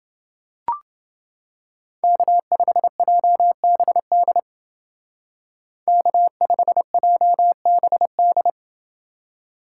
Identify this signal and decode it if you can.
Morse code, heard as a keyed tone, about 28 wpm, 700 Hz. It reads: K5JBD K5JBD